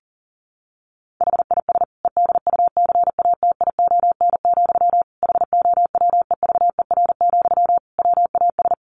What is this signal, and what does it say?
Morse code, heard as a keyed tone, about 40 wpm, 700 Hz. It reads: HIS EDUCATION, HOWEVER, WAS
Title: Sentences from Top 1400 Words 40wpm